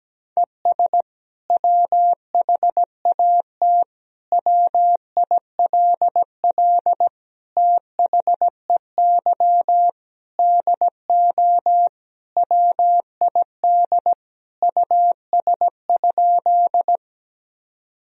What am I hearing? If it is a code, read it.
Morse code, heard as a keyed tone, about 17 wpm, 700 Hz. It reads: ES WHAT WILL THEY DO WID US?
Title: Sentences from Top 200 Words and Abbreviations 17wpm